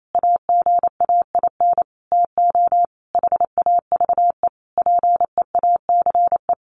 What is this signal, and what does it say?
Morse code, heard as a keyed tone, about 28 wpm, 700 Hz. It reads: AGAIN TO HAVE PEACE
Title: Sentences from Top 1500 Words 28wpm